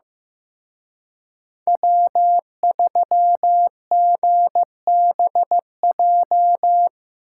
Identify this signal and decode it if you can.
Morse code, heard as a keyed tone, about 18 wpm, 700 Hz. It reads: W3GBJ
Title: Speed Racing - USA 1x3 Call Signs 15wpm